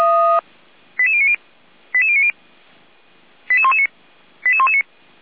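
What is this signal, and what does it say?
Five-tone selective call (ZVEI), heard as quick runs of five tones, tones of about 70 ms. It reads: 88090 88090 88119 88119
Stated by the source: ZVEI five-tone alert received by SDR, fire brigade feuerwehr-orlamuende.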